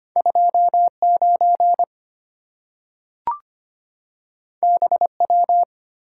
Morse code, heard as a keyed tone, about 25 wpm, 700 Hz. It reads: 29 BW
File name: Two Letter-Number Combinations - 25wpm